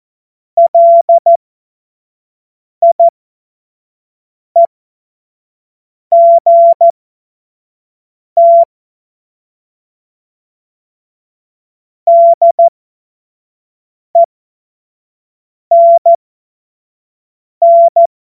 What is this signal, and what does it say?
Morse code, heard as a keyed tone, about 14 wpm, spaced as for 5 wpm, 680 Hz. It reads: LIEGT DENN
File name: Morsekurs / Witze / Witz 04. Juli 2023 25 F70